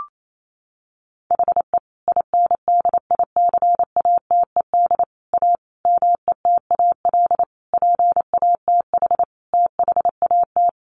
Morse code, heard as a keyed tone, about 28 wpm, 700 Hz. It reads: HE INDICATED A METAL PATH THAT
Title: A Sound Of Thunder 28wpm